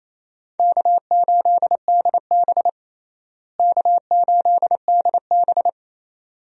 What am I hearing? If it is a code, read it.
Morse code, heard as a keyed tone, about 27 wpm, 700 Hz. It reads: K8DB K8DB
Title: USA Call Signs - 4 Characters - Repeated 2x - 28wpm